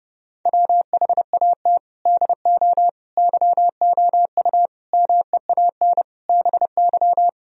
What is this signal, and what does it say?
Morse code, heard as a keyed tone, about 32 wpm, 700 Hz. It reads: WHAT DO YOU MEAN BY